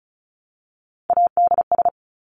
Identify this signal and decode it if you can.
Morse code, heard as a keyed tone, about 34 wpm, 700 Hz. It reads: ADS
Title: Abbreviations Repeated 3x - 35wpm